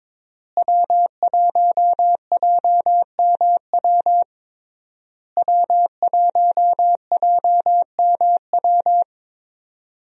Morse code, heard as a keyed tone, about 22 wpm, 700 Hz. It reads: W1JMW W1JMW
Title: USA Call Signs - 5 Characters - Repeated 3x - 22wpm